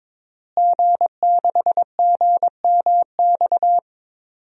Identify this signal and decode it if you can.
Morse code, heard as a keyed tone, about 22 wpm, 700 Hz. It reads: G6GMX
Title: DXCC-100 Call Signs - 5 Characters - Repeated 3x - 22wpm